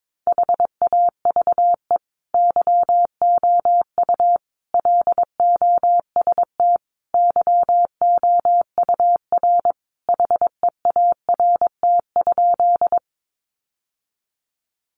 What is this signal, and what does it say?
Morse code, heard as a keyed tone, about 22 wpm, 700 Hz. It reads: HAVE YOU LOST YOUR HEART?
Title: Sentences from Top 1400 Words 22wpm